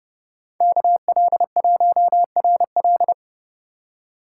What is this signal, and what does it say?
Morse code, heard as a keyed tone, about 30 wpm, 700 Hz. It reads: KL1RL